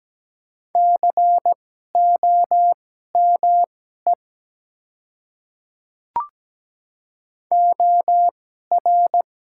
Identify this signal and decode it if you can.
Morse code, heard as a keyed tone, about 17 wpm, spaced as for 12 wpm, 700 Hz. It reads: COME OR
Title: 2x Character Spacing - Top 100 Words - 17wpm